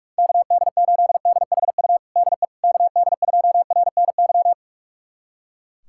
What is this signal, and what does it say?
Morse code, heard as a keyed tone, about 47 wpm, 700 Hz. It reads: KD8DHU DE KD2RNY
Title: USA Callsign DE Callsign 45wpm